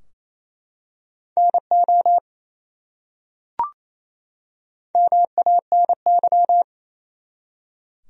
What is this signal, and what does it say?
Morse code, heard as a keyed tone, about 28 wpm, 700 Hz. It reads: NO MANY
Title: Top 100 Words 28wpm